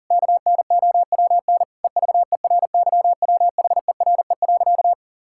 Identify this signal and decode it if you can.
Morse code, heard as a keyed tone, about 37 wpm, 700 Hz. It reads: KNOWN EVERYWHERE.
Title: Sentences from Top 1600 Words 40wpm